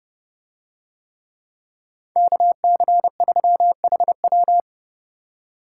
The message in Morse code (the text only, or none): KC3HW